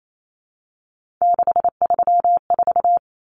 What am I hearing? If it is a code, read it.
Morse code, heard as a keyed tone, about 28 wpm, 700 Hz. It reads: 634
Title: Common Cut Numbers - 3-Digit 28wpm